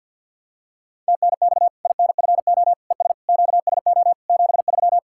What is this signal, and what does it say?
Morse code, heard as a keyed tone, about 51 wpm, 700 Hz. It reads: TNX INFO ES QSO 73
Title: Practice QSO - Exchange 4 - Ending - 50wpm